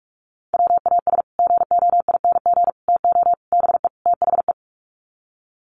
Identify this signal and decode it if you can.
Morse code, heard as a keyed tone, about 44 wpm, 690 Hz. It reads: WAS GOING TO BE THE
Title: Sets of 5 Words Encore2 45wpm